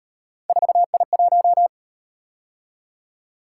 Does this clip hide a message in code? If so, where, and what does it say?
Yes: Morse code from 0.5 to 1.7 s, heard as a keyed tone, about 37 wpm, 700 Hz. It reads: 4I1